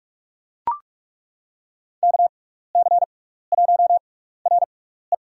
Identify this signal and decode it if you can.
Morse code, heard as a keyed tone, about 42 wpm, spaced as for 15 wpm, 700 Hz. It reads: KC1RE
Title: USA Call Signs - 6x Character Spacing - 45wpm